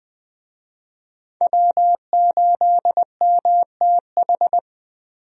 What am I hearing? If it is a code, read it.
Morse code, heard as a keyed tone, about 21 wpm, 700 Hz. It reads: W8MTH
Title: USA Call Signs - 5 Characters - Repeated 2x - 20wpm